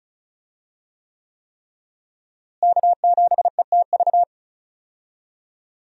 Morse code, heard as a keyed tone, about 36 wpm, 700 Hz. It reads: K7ETV
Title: USA Call Signs - 5 Characters - Repeated 3x - 35wpm